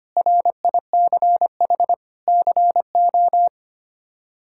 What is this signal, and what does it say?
Morse code, heard as a keyed tone, about 25 wpm, 700 Hz. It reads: RICH CO
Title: K1USN SST QSO Practice - 25wpm